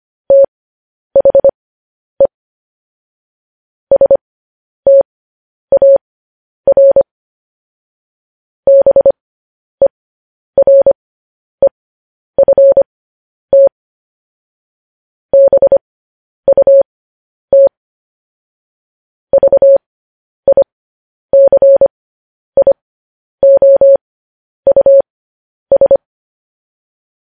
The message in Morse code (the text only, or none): THE STAR BEREFT BUT VICIOUS